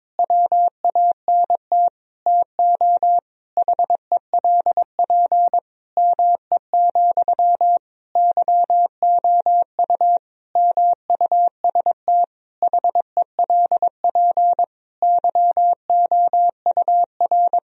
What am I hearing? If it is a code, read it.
Morse code, heard as a keyed tone, about 22 wpm, 700 Hz. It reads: WANT TO HELP ME, YOU MUST HELP YOUR
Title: Sentences from Top 1000 Words - Review Part 1 - 22wpm